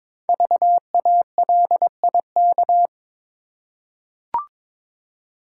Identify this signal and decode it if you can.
Morse code, heard as a keyed tone, about 23 wpm, 700 Hz. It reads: VALIK
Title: Code Groups - 5 Letters - 22wpm